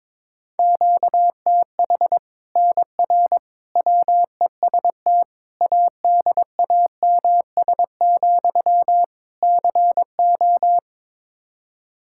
Morse code, heard as a keyed tone, about 22 wpm, 700 Hz. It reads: QTH NR WEST ADAMS, CO
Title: Practice QSO - Exchange 1 - Beginning - 22wpm